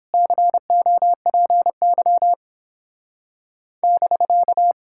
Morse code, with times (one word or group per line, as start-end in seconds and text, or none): 0.1-2.3 COPY
3.8-4.7 <BK>